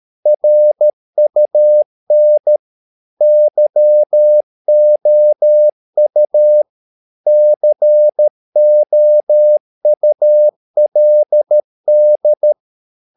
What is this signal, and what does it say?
Morse code, heard as a keyed tone, about 13 wpm, 600 Hz. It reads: RUN YOU COULD
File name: mp3 files / Lexiphiles - 13wpm